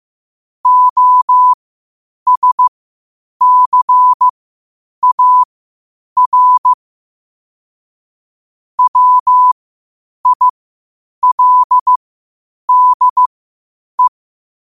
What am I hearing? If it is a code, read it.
Morse code, heard as a keyed tone, about 15 wpm, spaced as for 8 wpm, 1000 Hz. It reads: OSCAR WILDE
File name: qotdfiles / 8wpm